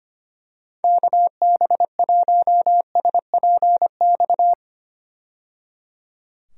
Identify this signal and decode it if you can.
Morse code, heard as a keyed tone, about 25 wpm, 700 Hz. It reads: KB1SPX
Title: USA Call Signs - 6 Characters - 25wpm